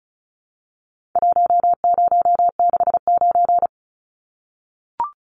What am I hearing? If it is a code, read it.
Morse code, heard as a keyed tone, about 34 wpm, 700 Hz. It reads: 1069
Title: Numbers - Contest Style - 4-Digit 35wpm